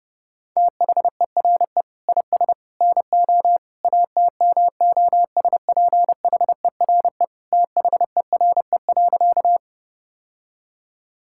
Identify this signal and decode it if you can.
Morse code, heard as a keyed tone, about 32 wpm, 700 Hz. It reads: THERE IS NO ATMOSPHERE THERE.
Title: Sentences from Top 1600 Words 30wpm